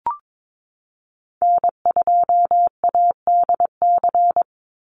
Morse code, heard as a keyed tone, about 22 wpm, 700 Hz. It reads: N2ADC
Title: Call Signs Repeated 3x - 22wpm